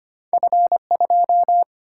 Morse code, heard as a keyed tone, about 25 wpm, 700 Hz. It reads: F2
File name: Two Letter-Number Combinations - 25wpm